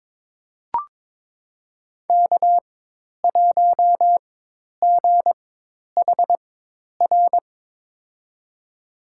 Morse code, heard as a keyed tone, about 22 wpm, spaced as for 10 wpm, 700 Hz. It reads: K1GHR
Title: USA Call Signs - 5 Characters - 4x Character Spacing - 22wpm